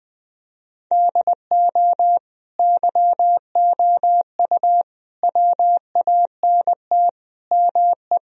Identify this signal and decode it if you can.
Morse code, heard as a keyed tone, about 20 wpm, 700 Hz. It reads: DO YOU WANT ME
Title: Speed-Racing - Sets of 5 Words 20wpm